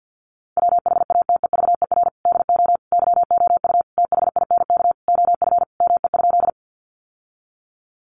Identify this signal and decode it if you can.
Morse code, heard as a keyed tone, about 50 wpm, 700 Hz. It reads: WHATEVER DO YOU THINK OF ME?
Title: Sentences from Top 1100 Words 50wpm